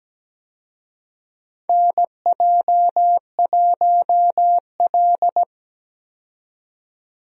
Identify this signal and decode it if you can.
Morse code, heard as a keyed tone, about 17 wpm, 700 Hz. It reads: NJ1L
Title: USA Call Signs - 4 Characters - 17wpm